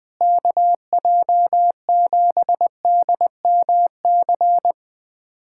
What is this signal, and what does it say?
Morse code, heard as a keyed tone, about 20 wpm, 700 Hz. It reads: KJ7DMC
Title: USA Callsign DE Callsign 20wpm